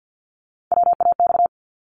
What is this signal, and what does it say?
Morse code, heard as a keyed tone, about 50 wpm, 690 Hz. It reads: WAX
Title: Multi-letter Phonemes Repeated 3x 50wpm